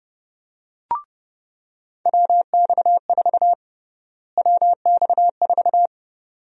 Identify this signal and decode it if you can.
Morse code, heard as a keyed tone, about 30 wpm, 700 Hz. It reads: WX4 WX4
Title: USA Call Sign Prefixes - Repeated 2x - 30wpm